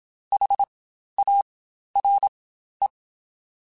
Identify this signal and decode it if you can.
Morse code, heard as a keyed tone, about 27 wpm, spaced as for 12 wpm, 800 Hz. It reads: HARE
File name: Spaced Alphas